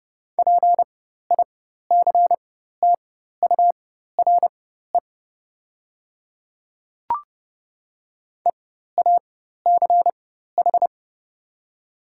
Morse code, heard as a keyed tone, about 30 wpm, spaced as for 14 wpm, 700 Hz. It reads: PICTURE EACH
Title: 4x Character Spacing - Top 300 Words - 30wpm